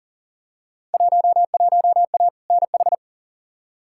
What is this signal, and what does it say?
Morse code, heard as a keyed tone, about 38 wpm, 700 Hz. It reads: 11A NH
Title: Field Day Exchanges Encore - 40wpm